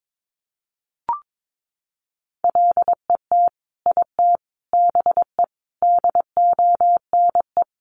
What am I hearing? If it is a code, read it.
Morse code, heard as a keyed tone, about 22 wpm, 700 Hz. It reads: LET IT BE DONE
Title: Sentences from Top 1100 Words 22wpm